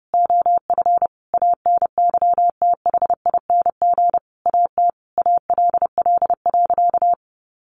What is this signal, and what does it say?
Morse code, heard as a keyed tone, about 30 wpm, 700 Hz. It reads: OF ANYTHING AT ALL.
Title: Sentences from Top 1300 Words 30wpm